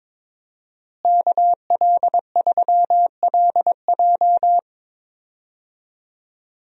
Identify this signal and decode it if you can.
Morse code, heard as a keyed tone, about 22 wpm, 700 Hz. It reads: KL3LJ
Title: USA Call Signs - 5 Characters - Repeated 2x - 22wpm